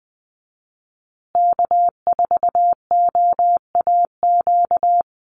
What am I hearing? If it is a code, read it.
Morse code, heard as a keyed tone, about 19 wpm, 700 Hz. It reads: K4OAQ
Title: Call Signs Repeated 3x - 20wpm